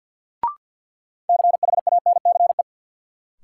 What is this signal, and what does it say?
Morse code, heard as a keyed tone, about 52 wpm, 700 Hz. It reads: CHANCE
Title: Sets of 3 Words - New Word Drill - 50wpm